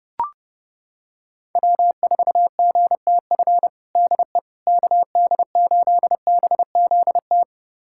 Morse code, heard as a keyed tone, about 32 wpm, 700 Hz. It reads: W4GTF DE KD8BZT
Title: USA Callsign DE Callsign 30wpm